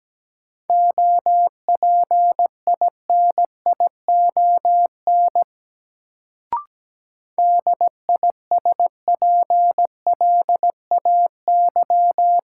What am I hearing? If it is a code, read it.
Morse code, heard as a keyed tone, about 17 wpm, 700 Hz. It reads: OPINION DISPLAY
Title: Sentences from Top 1200 Words - New Word Drill Alt - 17wpm